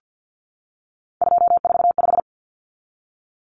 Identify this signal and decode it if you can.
Morse code, heard as a keyed tone, about 50 wpm, 700 Hz. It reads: 245
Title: Common Cut Numbers - 3-Digit 50wpm